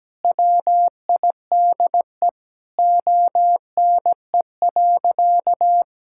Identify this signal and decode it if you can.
Morse code, heard as a keyed tone, about 17 wpm, 700 Hz. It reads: WIDE ONE.
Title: Sentences from Top 1000 Words - Review Part 6 - 17wpm